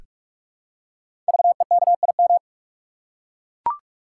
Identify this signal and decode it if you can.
Morse code, heard as a keyed tone, about 44 wpm, 700 Hz. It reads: VEKIM
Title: Code Groups - 5 Letters - 45wpm